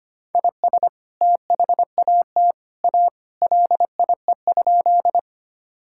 Morse code, heard as a keyed tone, about 25 wpm, 700 Hz. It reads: IS THAT A LIE?
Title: Sentences from Top 1600 Words 25wpm